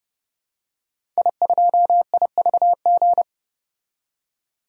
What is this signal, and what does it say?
Morse code, heard as a keyed tone, about 30 wpm, 700 Hz. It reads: I2IVG
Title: DXCC-100 Call Signs - 5 Characters - 30wpm